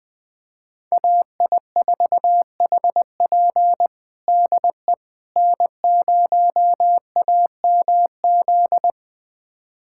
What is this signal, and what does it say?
Morse code, heard as a keyed tone, about 21 wpm, 700 Hz. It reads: AI4HP DE N0AMZ
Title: USA Callsign DE Callsign - 5 Characters - 20wpm